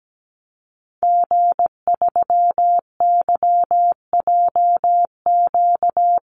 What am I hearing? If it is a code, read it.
Morse code, heard as a keyed tone, about 17 wpm, 700 Hz. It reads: G3YJQ
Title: Call Signs - 17wpm